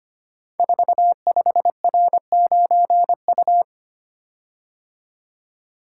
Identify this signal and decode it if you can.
Morse code, heard as a keyed tone, about 25 wpm, 700 Hz. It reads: 45R9U
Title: Code Groups - 5 Letter:Numbers - 25wpm